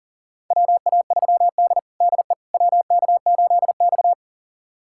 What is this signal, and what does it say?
Morse code, heard as a keyed tone, about 40 wpm, 700 Hz. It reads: WA3D DE WK8X
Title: USA Callsign DE Callsign - 4 Characters - 40wpm